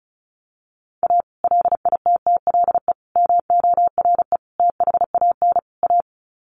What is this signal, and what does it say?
Morse code, heard as a keyed tone, about 34 wpm, 700 Hz. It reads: A LITTLE MORE THAN A
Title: Sets of 5 Words Encore2 35wpm